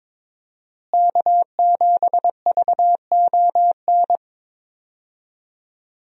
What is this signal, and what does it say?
Morse code, heard as a keyed tone, about 22 wpm, 700 Hz. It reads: K7VON
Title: USA Call Signs - 5 Characters - Repeated 2x - 22wpm